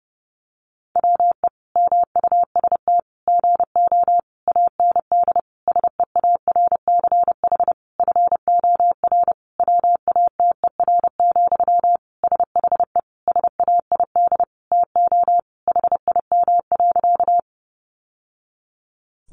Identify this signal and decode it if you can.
Morse code, heard as a keyed tone, about 30 wpm, 700 Hz. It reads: WE MUST GO AND SEARCH FOR WATER, SHE SAID TO HIM.
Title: Sentences from Top 800 Words 30wpm